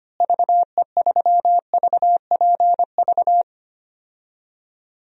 Morse code, heard as a keyed tone, about 25 wpm, 700 Hz. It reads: VE3VPV
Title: DXCC-100 Call Signs - 6 Characters - Repeated 3x - 25wpm